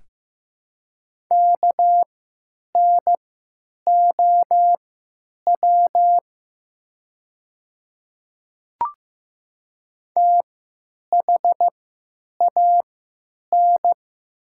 Morse code, heard as a keyed tone, about 15 wpm, spaced as for 9 wpm, 700 Hz. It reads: KNOW THAN
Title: 3x Character Spacing - Top 100 Words - 15wpm